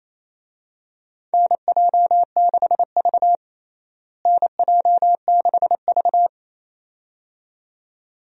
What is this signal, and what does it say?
Morse code, heard as a keyed tone, about 28 wpm, 700 Hz. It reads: NJ6V NJ6V
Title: USA Call Signs - 4 Characters - Repeated 2x - 28wpm